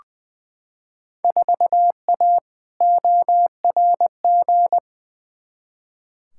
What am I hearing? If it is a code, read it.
Morse code, heard as a keyed tone, about 20 wpm, 700 Hz. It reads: 4A ORG
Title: Field Day Exchanges - 20wpm